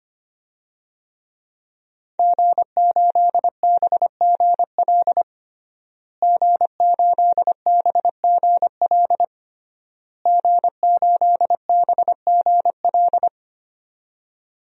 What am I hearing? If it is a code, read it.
Morse code, heard as a keyed tone, about 25 wpm, 700 Hz. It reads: G8BGL G8BGL G8BGL
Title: DXCC-100 Call Signs - 5 Characters - Repeated 3x - 25wpm